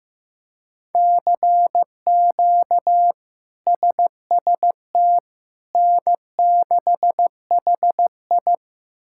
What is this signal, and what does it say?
Morse code, heard as a keyed tone, about 15 wpm, 700 Hz. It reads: CQ SST N6HI
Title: K1USN SST QSO Practice - 15wpm